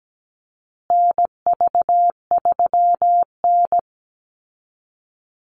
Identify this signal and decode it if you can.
Morse code, heard as a keyed tone, about 17 wpm, 700 Hz. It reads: NV3N
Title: Call Signs Repeated 2x - 17wpm